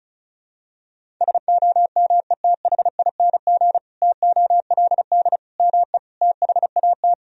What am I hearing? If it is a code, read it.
Morse code, heard as a keyed tone, about 36 wpm, 700 Hz. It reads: SOMETHING TOLD ME THAT
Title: Sentences from Top 1000 Words - Review Part 1 - 35wpm